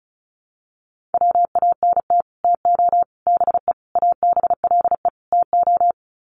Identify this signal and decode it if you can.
Morse code, heard as a keyed tone, about 34 wpm, 700 Hz. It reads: WANT TO BE ABLE TO
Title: Sets of 5 Words 35wpm